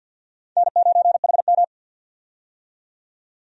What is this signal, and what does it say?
Morse code, heard as a keyed tone, about 53 wpm, 700 Hz. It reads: N9HM
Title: USA Call Signs - 4 Characters - 50wpm